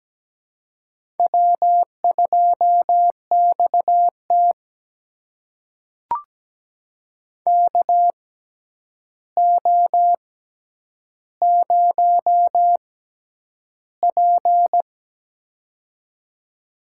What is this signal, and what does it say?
Morse code, heard as a keyed tone, about 17 wpm, spaced as for 9 wpm, 700 Hz. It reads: W2XT KO0P
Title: USA Call Signs - 4 Characters - 6x Character Spacing - 17wpm